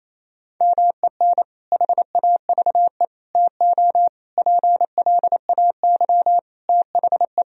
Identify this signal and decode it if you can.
Morse code, heard as a keyed tone, about 28 wpm, 700 Hz. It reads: MEN HAVE TO PLAY THE
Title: Sentences from Top 1000 Words - Review Part 3 - 28wpm